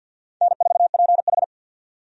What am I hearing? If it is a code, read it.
Morse code, heard as a keyed tone, about 50 wpm, 700 Hz. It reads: N4PH
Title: USA Call Signs - 4 Characters - 50wpm